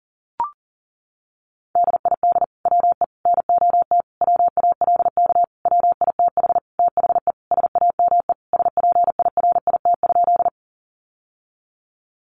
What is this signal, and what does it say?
Morse code, heard as a keyed tone, about 40 wpm, 700 Hz. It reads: DID WE NOT WALK WITH THE SAME SPIRIT?
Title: Sentences from Top 1400 Words 40wpm